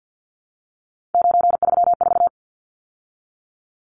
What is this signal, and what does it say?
Morse code, heard as a keyed tone, about 50 wpm, 700 Hz. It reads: 934
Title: Numbers - 3-Digit 50wpm